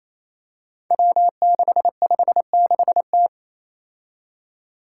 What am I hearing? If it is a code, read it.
Morse code, heard as a keyed tone, about 28 wpm, 700 Hz. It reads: W656T
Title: Five Challenging Letter-Number Combinations 28wpm